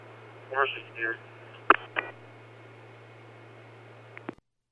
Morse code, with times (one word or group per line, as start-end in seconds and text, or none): none